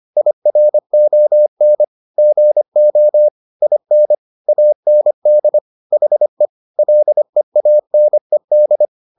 Morse code, heard as a keyed tone, about 25 wpm, 600 Hz. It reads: IRON GO IN AND HE LEANED